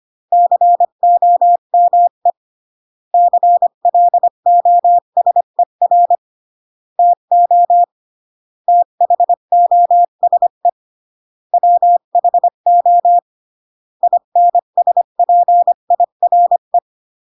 Morse code, heard as a keyed tone, about 25 wpm, 700 Hz. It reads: COME CLOSER TO THOSE WHO INSPIRE